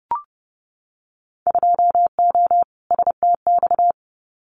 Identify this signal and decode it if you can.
Morse code, heard as a keyed tone, about 30 wpm, 700 Hz. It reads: 2O STX
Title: Winter Field Day Exchanges 30wpm